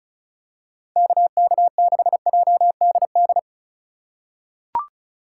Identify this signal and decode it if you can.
Morse code, heard as a keyed tone, about 36 wpm, 700 Hz. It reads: KK6JDD
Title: USA Call Signs - 6 Characters - 2x Character Spacing - 35wpm